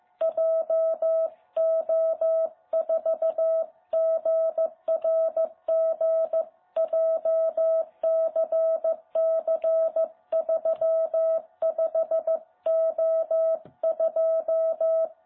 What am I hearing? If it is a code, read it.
Morse code, heard as a keyed tone, about 14 wpm, 660 Hz. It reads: JO4GRG JCC35O2
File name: jh8cbh / 2021cw